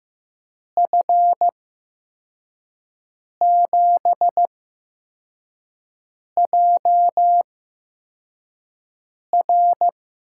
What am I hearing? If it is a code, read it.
Morse code, heard as a keyed tone, about 15 wpm, spaced as for 4 wpm, 700 Hz. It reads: F7JR
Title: USA Call Signs - 5 Characters - 8x Character Spacing - 15wpm